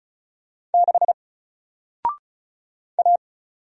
Morse code, heard as a keyed tone, about 36 wpm, 700 Hz. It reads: 6 A
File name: Single-Letter-Number-35wpm